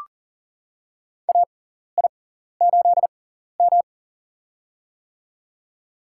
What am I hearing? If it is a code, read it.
Morse code, heard as a keyed tone, about 42 wpm, spaced as for 14 wpm, 700 Hz. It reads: AI8M
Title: USA Call Signs - 4 Characters - 6x Character Spacing - 40wpm